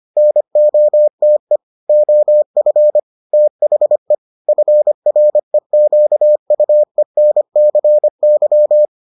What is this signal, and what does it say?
Morse code, heard as a keyed tone, about 25 wpm, 600 Hz. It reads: NOTE OF THE FREQUENCY